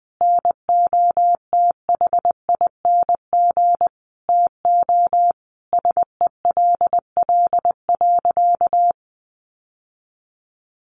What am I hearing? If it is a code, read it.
Morse code, heard as a keyed tone, about 20 wpm, 700 Hz. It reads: NOTHING TO SELL.
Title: Sentences from Top 1100 Words 20wpm